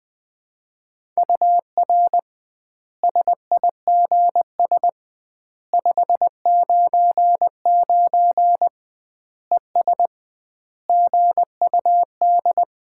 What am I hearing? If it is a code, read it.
Morse code, heard as a keyed tone, about 21 wpm, 700 Hz. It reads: UR SIGS 599 ES GUD
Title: Common QSO Phrases - 2x Word-Spacing - 20wpm